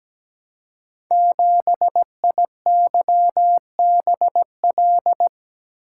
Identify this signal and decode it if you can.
Morse code, heard as a keyed tone, about 17 wpm, 700 Hz. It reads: 7IYBL
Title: Code Groups - 5 Letter:Numbers - 17wpm